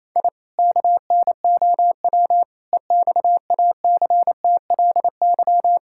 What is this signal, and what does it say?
Morse code, heard as a keyed tone, about 28 wpm, 700 Hz. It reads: I KNOW EXACTLY
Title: Sentences from Top 1000 Words - Review Part 6 - 28wpm